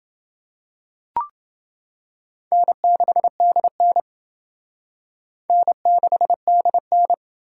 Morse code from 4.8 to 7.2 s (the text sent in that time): N6DN